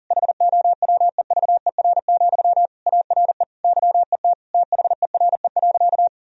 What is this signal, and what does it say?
Morse code, heard as a keyed tone, about 40 wpm, 700 Hz. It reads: HOWEVER, ARE YET THERE.
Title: Sentences from Top 1000 Words - Review Part 1 - 40wpm